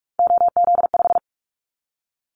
Morse code, heard as a keyed tone, about 44 wpm, 700 Hz. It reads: OZ5
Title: Callsign Prefixes for Top 50 DXCC - 45wpm